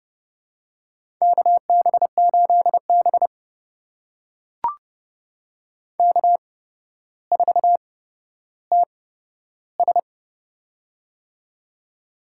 Morse code, heard as a keyed tone, about 30 wpm, spaced as for 13 wpm, 700 Hz. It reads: KB8B K4TS
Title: USA Call Signs - 8x Character Spacing - 30wpm